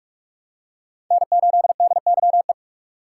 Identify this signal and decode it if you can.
Morse code, heard as a keyed tone, about 47 wpm, 700 Hz. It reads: N8DYE
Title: USA Call Signs - 5 Characters - 4x Character Spacing - 45wpm